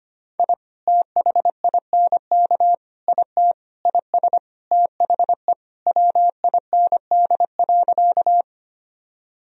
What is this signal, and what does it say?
Morse code, heard as a keyed tone, about 25 wpm, 700 Hz. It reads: I THINK IT IS THE WIND.
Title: Sentences from Top 1000 Words - Review Part 2 - 25wpm